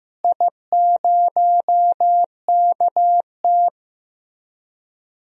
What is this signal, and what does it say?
Morse code, heard as a keyed tone, about 15 wpm, 700 Hz. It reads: I0KT
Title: DXCC-100 Call Signs - 4 Characters - Repeated 3x - 15wpm